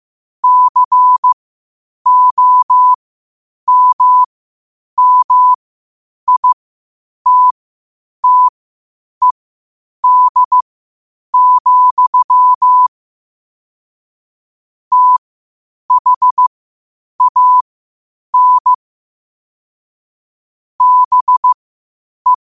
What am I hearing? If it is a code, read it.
Morse code, heard as a keyed tone, about 15 wpm, spaced as for 8 wpm, 1000 Hz. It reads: COMMITTED, THAN BE
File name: 8wpm